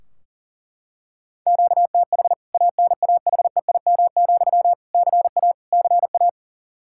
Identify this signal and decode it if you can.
Morse code, heard as a keyed tone, about 40 wpm, 700 Hz. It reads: QTH ANAHEIM, CA CA
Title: K1USN SST QSO Practice 40wpm